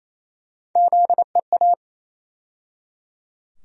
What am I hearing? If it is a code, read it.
Morse code, heard as a keyed tone, about 28 wpm, 700 Hz. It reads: ZEA
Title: USA Call Sign Postfixes - 3 Letters 28wpm